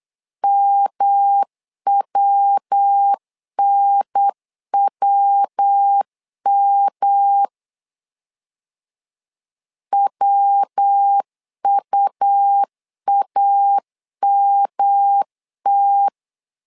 Morse code, heard as a keyed tone, about 8 wpm, 800 Hz. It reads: MWNWM WUAMT
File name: LEZIONI / 3 MWRO / 3 MW 8wpm 2501